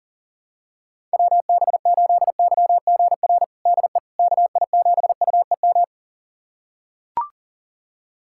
Morse code, heard as a keyed tone, about 37 wpm, 700 Hz. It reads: WB8YGR DE KI7UEM
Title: USA Callsign DE Callsign - 6 Characters - 40wpm